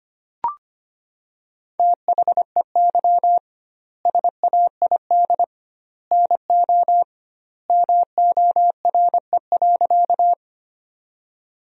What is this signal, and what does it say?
Morse code, heard as a keyed tone, about 25 wpm, 700 Hz. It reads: THEY SAID NO MORE.
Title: 2x Word-Spacing - Sentences from Top 100 Words - 25wpm